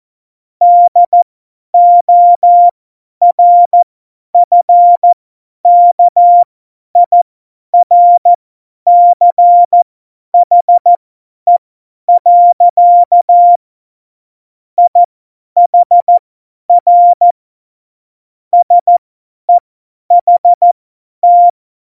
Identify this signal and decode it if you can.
Morse code, heard as a keyed tone, about 14 wpm, spaced as for 10 wpm, 700 Hz. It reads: DORFKIRCHE. IHR SEHT
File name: Morsekurs / Witze / Witz 01. Dezember 2020, 50 F70